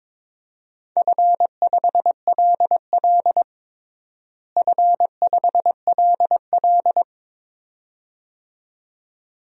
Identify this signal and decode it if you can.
Morse code, heard as a keyed tone, about 22 wpm, 700 Hz. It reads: F5LL F5LL
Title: DXCC-100 Call Signs - 4 Characters - Repeated 2x - 22wpm